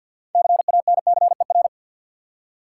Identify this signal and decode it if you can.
Morse code, heard as a keyed tone, about 53 wpm, 700 Hz. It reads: CANCER
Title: Six Letter Words - 50wpm